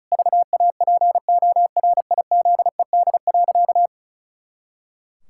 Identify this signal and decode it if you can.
Morse code, heard as a keyed tone, about 34 wpm, 700 Hz. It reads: VAPORIZED.
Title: Pangram Challenge v2 - 35wpm